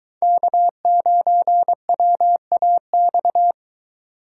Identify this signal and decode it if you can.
Morse code, heard as a keyed tone, about 21 wpm, 700 Hz. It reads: K9WAX